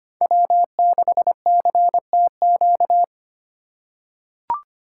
Morse code, heard as a keyed tone, about 25 wpm, 700 Hz. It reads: W6CTQ
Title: USA Call Signs - 5 Characters - 2x Character Spacing - 25wpm